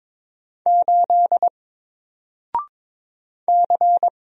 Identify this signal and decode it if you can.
Morse code, heard as a keyed tone, about 22 wpm, 700 Hz. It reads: 8 C
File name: Single-Letter-Number-22wpm